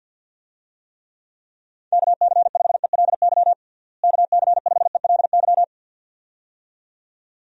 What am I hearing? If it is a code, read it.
Morse code, heard as a keyed tone, about 51 wpm, 700 Hz. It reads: KC5ELY KC5ELY
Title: USA Call Signs - 6 Characters - Repeated 2x - 50wpm